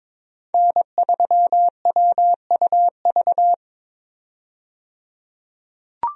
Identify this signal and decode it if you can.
Morse code, heard as a keyed tone, about 22 wpm, 700 Hz. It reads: N3WUV
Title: USA Call Signs - 5 Characters - 22wpm